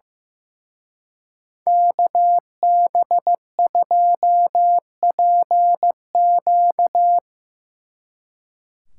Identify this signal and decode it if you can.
Morse code, heard as a keyed tone, about 17 wpm, 700 Hz. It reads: KB2PQ
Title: Speed Racing - USA 2x2 Call Signs 15wpm